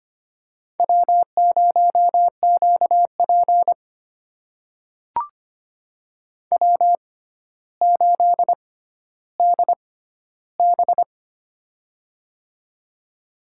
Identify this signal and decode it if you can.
Morse code, heard as a keyed tone, about 25 wpm, spaced as for 13 wpm, 700 Hz. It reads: W0QP W8DB
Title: USA Call Signs - 4 Characters - 6x Character Spacing - 25wpm